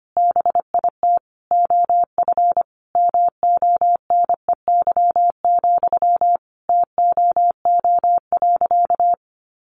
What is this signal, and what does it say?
Morse code, heard as a keyed tone, about 25 wpm, 700 Hz. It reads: BIT OF MONEY, TOO.
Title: Sentences from Top 900 Words 25wpm